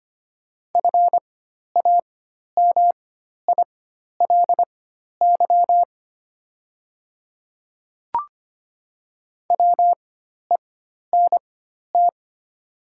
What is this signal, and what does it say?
Morse code, heard as a keyed tone, about 25 wpm, spaced as for 12 wpm, 690 Hz. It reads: FAMILY WENT
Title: 4x Character Spacing - Top 500 Words - 25wpm